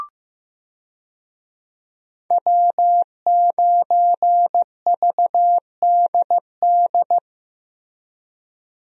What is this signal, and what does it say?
Morse code, heard as a keyed tone, about 15 wpm, 700 Hz. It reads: W9VDD
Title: USA Call Signs - 5 Characters - Repeated 2x - 15wpm